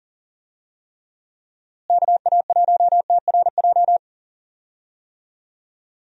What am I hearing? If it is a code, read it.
Morse code, heard as a keyed tone, about 40 wpm, 700 Hz. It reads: KA1TRJ
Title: USA Call Signs - 6 Characters - Repeated 2x - 40wpm